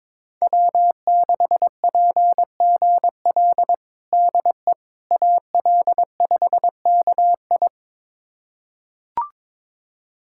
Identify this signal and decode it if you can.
Morse code, heard as a keyed tone, about 22 wpm, 700 Hz. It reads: W6PGL DE AL5KI
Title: USA Callsign DE Callsign - 5 Characters - 22wpm